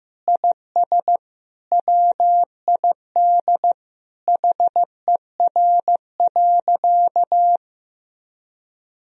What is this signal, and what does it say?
Morse code, heard as a keyed tone, about 15 wpm, 700 Hz. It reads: IS WID HER.
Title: Sentences from Top 100 Words and Abbreviations 15wpm